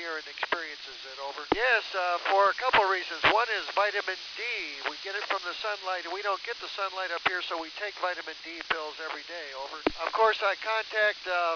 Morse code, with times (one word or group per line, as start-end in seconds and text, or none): none